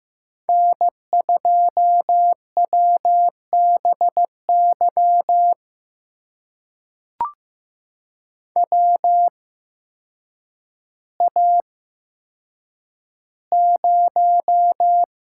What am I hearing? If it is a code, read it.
Morse code, heard as a keyed tone, about 15 wpm, spaced as for 8 wpm, 700 Hz. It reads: N2WBY WA0
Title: USA Call Signs - 5 Characters - 8x Character Spacing - 15wpm